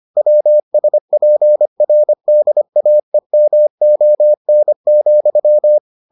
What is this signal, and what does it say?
Morse code, heard as a keyed tone, about 25 wpm, 600 Hz. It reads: WSPRDAEMON,